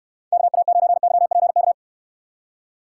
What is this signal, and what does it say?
Morse code, heard as a keyed tone, about 70 wpm, 700 Hz. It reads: KA9YPK